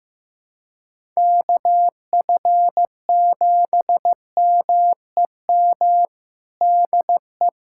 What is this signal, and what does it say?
Morse code, heard as a keyed tone, about 16 wpm, 700 Hz. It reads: KF7MEM DE
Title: USA Callsign DE Callsign - 6 Characters - 15wpm